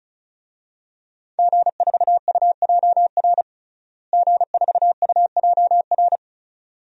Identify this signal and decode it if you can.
Morse code, heard as a keyed tone, about 36 wpm, 700 Hz. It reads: G4UJR G4UJR
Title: DXCC-100 Call Signs - 5 Characters - Repeated 2x - 35wpm